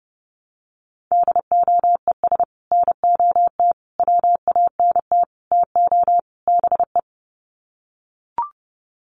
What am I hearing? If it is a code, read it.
Morse code, heard as a keyed tone, about 30 wpm, 700 Hz. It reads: DOES NOT WANT TO BE
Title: Sets of 5 Words Encore 30wpm